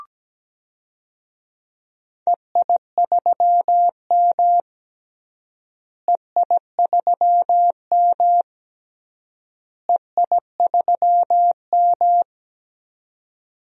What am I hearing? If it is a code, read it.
Morse code, heard as a keyed tone, about 17 wpm, 700 Hz. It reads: EI3M EI3M EI3M
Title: DXCC-100 Call Signs - 4 Characters - Repeated 3x - 17wpm